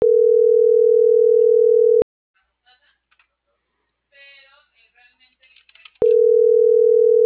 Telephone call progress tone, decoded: ringback tone

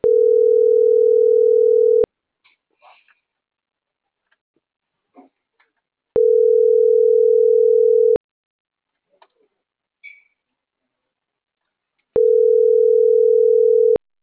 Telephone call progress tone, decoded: ringback tone

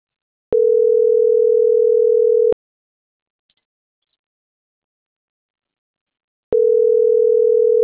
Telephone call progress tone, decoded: ringback tone

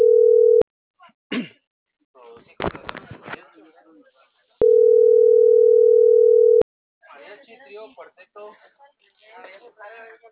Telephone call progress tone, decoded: ringback tone

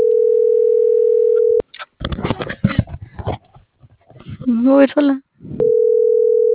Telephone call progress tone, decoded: ringback tone

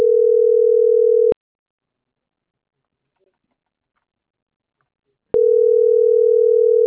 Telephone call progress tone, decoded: ringback tone

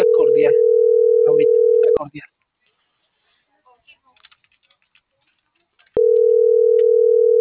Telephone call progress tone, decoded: ringback tone

